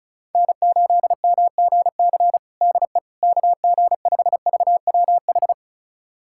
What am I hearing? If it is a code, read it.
Morse code, heard as a keyed tone, about 36 wpm, 700 Hz. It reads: N8MGC DE KG5VWH
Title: USA Callsign DE Callsign 35wpm